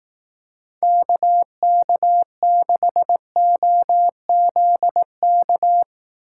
Morse code, heard as a keyed tone, about 18 wpm, 700 Hz. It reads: KK6OZK
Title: Speed Racing - USA 2x3 Call Signs 15wpm